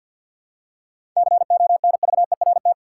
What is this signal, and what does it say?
Morse code, heard as a keyed tone, about 52 wpm, 700 Hz. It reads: CONVERT
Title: Seven Letter Words - 50wpm